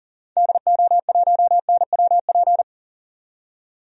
Morse code, heard as a keyed tone, about 40 wpm, 700 Hz. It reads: DO1NWP
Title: DXCC-100 Call Signs - 6 Characters - 40wpm